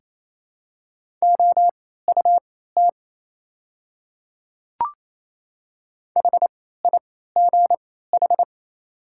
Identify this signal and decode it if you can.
Morse code, heard as a keyed tone, about 27 wpm, spaced as for 16 wpm, 700 Hz. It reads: OUT HIGH
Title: 3x Character Spacing - Top 300 Words - 28wpm